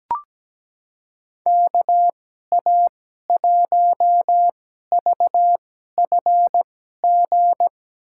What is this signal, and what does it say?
Morse code, heard as a keyed tone, about 17 wpm, spaced as for 12 wpm, 700 Hz. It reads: KA1VFG
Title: USA Call Signs - 6 Characters - 2x Character Spacing - 17wpm